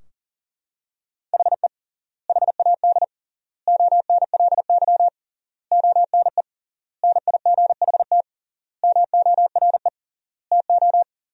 Morse code, heard as a keyed tone, about 40 wpm, 700 Hz. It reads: HE HAD ONLY ONE NIGHT MORE TO
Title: 3x Word-Spacing - Sentences from Top 300 Words - 40wpm